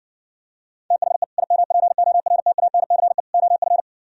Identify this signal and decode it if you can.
Morse code, heard as a keyed tone, about 69 wpm, 700 Hz. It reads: THE IMPORTANCE OF